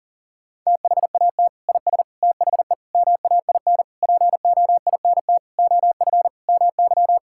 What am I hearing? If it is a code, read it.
Morse code, heard as a keyed tone, about 40 wpm, 700 Hz. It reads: THAT IS THE MAIN POINT OF MY
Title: Sentences from Top 1000 Words - Review Part 5 - 40wpm